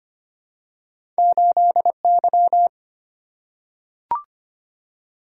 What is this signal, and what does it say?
Morse code, heard as a keyed tone, about 25 wpm, 700 Hz. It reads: 8Y
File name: Two Letter-Number Combinations - 25wpm